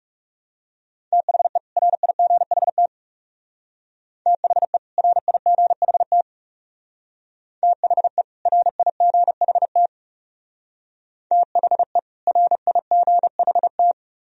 Morse code, heard as a keyed tone, about 39 wpm, 700 Hz. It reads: THE RIGHT THE RIGHT THE RIGHT THE RIGHT